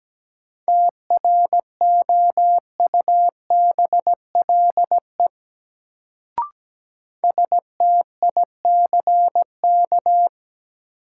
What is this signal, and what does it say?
Morse code, heard as a keyed tone, about 17 wpm, 700 Hz. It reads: TROUBLE STICK
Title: Sentences from Top 1200 Words - New Word Drill Alt - 17wpm